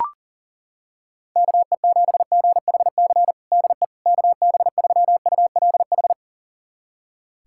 Morse code, heard as a keyed tone, about 40 wpm, 700 Hz. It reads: KE7GHC DE KB3ULH
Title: USA Callsign DE Callsign - 6 Characters - 40wpm